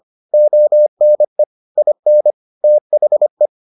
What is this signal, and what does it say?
Morse code, heard as a keyed tone, about 25 wpm, 600 Hz. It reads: ONE IN THE